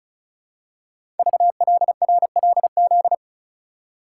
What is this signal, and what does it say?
Morse code, heard as a keyed tone, about 36 wpm, 700 Hz. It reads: VLRLZ
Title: Code Groups - 5 Letters - 35wpm